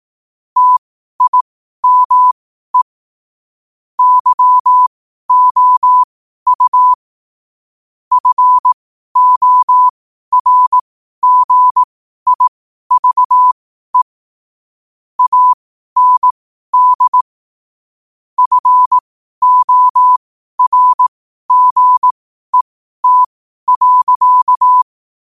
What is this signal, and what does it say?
Morse code, heard as a keyed tone, about 18 wpm, spaced as for 12 wpm, 1000 Hz. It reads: TIME YOU FORGIVE AND FORGET.
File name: qotdfiles / 12wpm